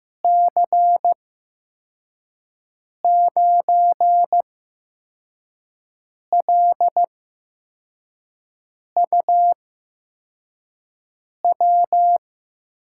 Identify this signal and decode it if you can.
Morse code, heard as a keyed tone, about 15 wpm, spaced as for 4 wpm, 700 Hz. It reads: C9LUW
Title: USA Call Signs - 6 Characters - 8x Character Spacing - 15wpm